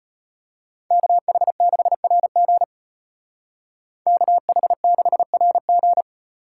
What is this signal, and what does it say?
Morse code, heard as a keyed tone, about 31 wpm, 700 Hz. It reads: KH6RG KH6RG